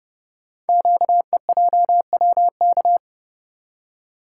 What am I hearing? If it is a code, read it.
Morse code, heard as a keyed tone, about 30 wpm, 690 Hz. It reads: QEJWK
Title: Code Groups - 5 Letters - 30wpm